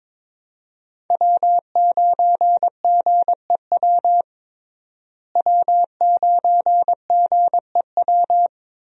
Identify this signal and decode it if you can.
Morse code, heard as a keyed tone, about 21 wpm, 700 Hz. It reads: W9GEW W9GEW
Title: USA Call Signs - 5 Characters - Repeated 2x - 22wpm